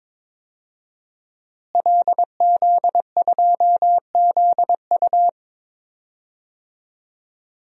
Morse code, heard as a keyed tone, about 22 wpm, 700 Hz. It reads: LZ2ZU
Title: DXCC-100 Call Signs - 5 Characters - Repeated 2x - 22wpm